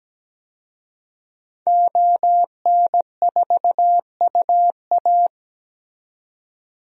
Morse code, heard as a keyed tone, about 17 wpm, 700 Hz. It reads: ON4UA
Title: DXCC-100 Call Signs - 5 Characters - 17wpm